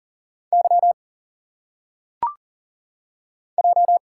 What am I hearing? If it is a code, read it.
Morse code, heard as a keyed tone, about 40 wpm, 700 Hz. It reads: Y J
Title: Single Letters - Phonetic Pronunciation 40wpm